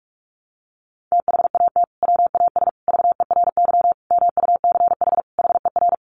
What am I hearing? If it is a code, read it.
Morse code, heard as a keyed tone, about 44 wpm, 700 Hz. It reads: THAT WAS VERY MUCH HER
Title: Sentences from Top 1200 Words 45wpm